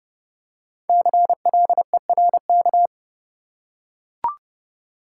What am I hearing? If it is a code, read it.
Morse code, heard as a keyed tone, about 30 wpm, 700 Hz. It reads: CLERK
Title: Five Letter Words - 30wpm